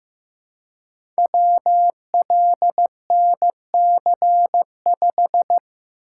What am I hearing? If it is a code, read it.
Morse code, heard as a keyed tone, about 15 wpm, 700 Hz. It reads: WLNC5
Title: Code Groups - 5 Letter:Numbers - 15wpm